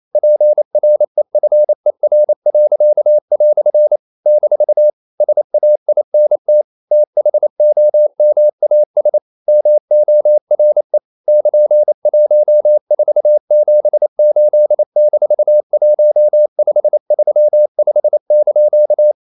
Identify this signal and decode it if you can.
Morse code, heard as a keyed tone, about 28 wpm, 600 Hz. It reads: PREFER." - SAINT THOMAS MORE <KN>1478-1535)
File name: CW / Practice / EnglishQuotations0128